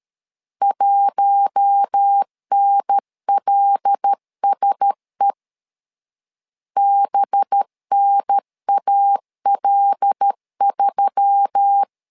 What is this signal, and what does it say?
Morse code, heard as a keyed tone, about 13 wpm, 800 Hz. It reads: 1NLSE BNAL3
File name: LEZIONI / 5 LBVC / 5 LB 12wpm 1906